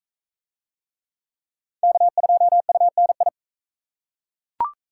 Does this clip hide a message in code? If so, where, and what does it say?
Yes: Morse code from 1.8 to 3.3 s, heard as a keyed tone, about 41 wpm, 700 Hz. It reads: K2UNI